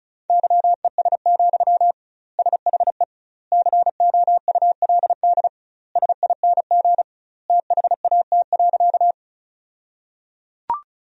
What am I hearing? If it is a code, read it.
Morse code, heard as a keyed tone, about 34 wpm, 700 Hz. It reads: YES, SHE COULD SING THAT.
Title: 2x Word-Spacing - Sentences from Top 500 Words - 35wpm